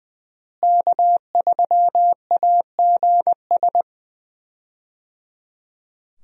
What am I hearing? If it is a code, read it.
Morse code, heard as a keyed tone, about 20 wpm, 700 Hz. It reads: K3AGS
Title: USA Call Signs - 5 Characters - Repeated 3x - 20wpm